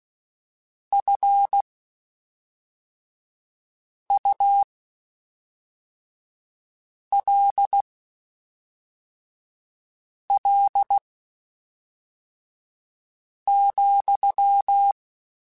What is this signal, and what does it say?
Morse code, heard as a keyed tone, about 16 wpm, spaced as for 3 wpm, 800 Hz. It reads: FULL,
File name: Spaced / Chuckles Spaced-mp3s / HalfFull sp @5